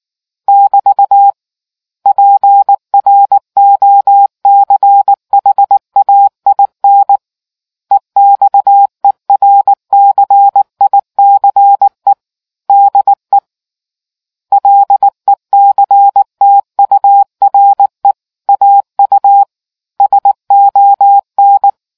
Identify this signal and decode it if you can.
Morse code, heard as a keyed tone, about 19 wpm, 800 Hz. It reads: <BT> PROCHAIN EXERCICE DE LECTURE AU SON